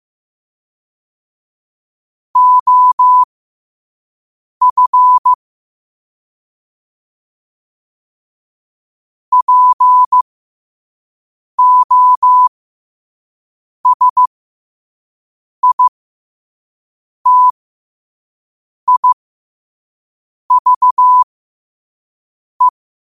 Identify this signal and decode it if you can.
Morse code, heard as a keyed tone, about 15 wpm, spaced as for 5 wpm, 1000 Hz. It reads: OF POSITIVE
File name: qotdfiles / 5wpm